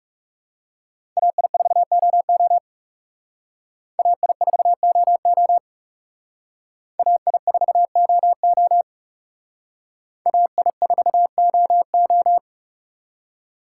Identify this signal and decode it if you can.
Morse code, heard as a keyed tone, about 38 wpm, 700 Hz. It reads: AI4OO AI4OO AI4OO AI4OO
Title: Speed Racing - USA Call Signs 30wpm